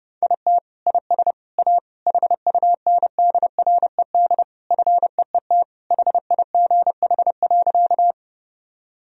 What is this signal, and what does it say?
Morse code, heard as a keyed tone, about 31 wpm, 700 Hz. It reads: IT IS A HUNDRED FEET HIGH.
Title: Sentences from Top 1000 Words - Review Part 2 - 30wpm